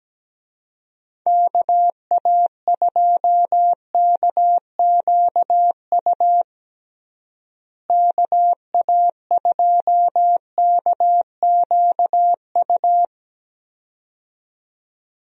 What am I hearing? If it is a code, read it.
Morse code, heard as a keyed tone, about 17 wpm, 700 Hz. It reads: KA2KQU KA2KQU
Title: USA Call Signs - 6 Characters - Repeated 2x - 17wpm